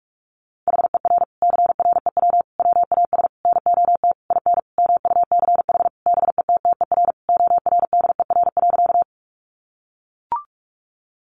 Morse code, heard as a keyed tone, about 44 wpm, 700 Hz. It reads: HER CREW WAS NOT IN MUCH BETTER ORDER.
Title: Sentences from Top 1500 Words 45wpm